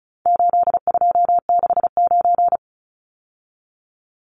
Morse code, heard as a keyed tone, about 34 wpm, 700 Hz. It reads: 8269
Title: Numbers - 4-Digit 35wpm